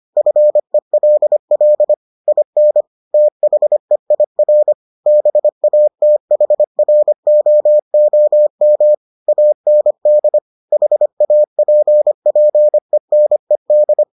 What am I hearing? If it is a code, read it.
Morse code, heard as a keyed tone, about 25 wpm, 600 Hz. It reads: FELL IN THEIR BATHROOM AND HAPPENED